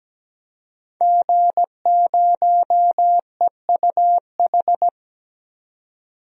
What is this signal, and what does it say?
Morse code, heard as a keyed tone, about 17 wpm, 700 Hz. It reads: G0EUH
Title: DXCC-100 Call Signs - 5 Characters - 17wpm